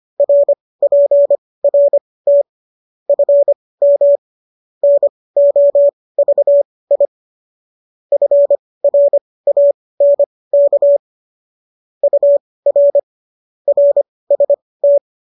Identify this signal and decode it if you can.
Morse code, heard as a keyed tone, about 25 wpm, spaced as for 18 wpm, 570 Hz. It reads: RPRT FM NOVI FRANK UR RST